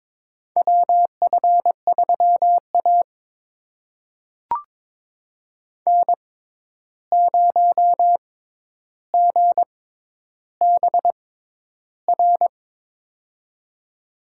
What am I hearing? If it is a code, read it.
Morse code, heard as a keyed tone, about 22 wpm, spaced as for 11 wpm, 700 Hz. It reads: WF3A N0GBR
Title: USA Call Signs - 6x Character Spacing - 22wpm